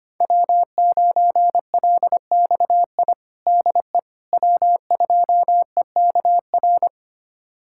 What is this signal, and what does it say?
Morse code, heard as a keyed tone, about 25 wpm, 700 Hz. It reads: W9LXI DE W2EKR
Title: USA Callsign DE Callsign - 5 Characters - 25wpm